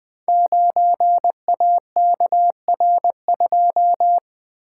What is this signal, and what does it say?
Morse code, heard as a keyed tone, about 20 wpm, 700 Hz. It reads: 9AKR2
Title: Code Groups - 5 Letter:Numbers - 20wpm